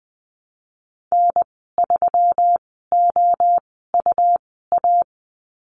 Morse code, heard as a keyed tone, about 20 wpm, spaced as for 14 wpm, 700 Hz. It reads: N3OUA
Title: 128 Lesson 21g - Number 6 - Call Signs 20wpm